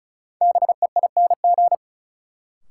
Morse code, heard as a keyed tone, about 36 wpm, 700 Hz. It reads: BEING
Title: Sets of 4 Words - New Word Drill - 35wpm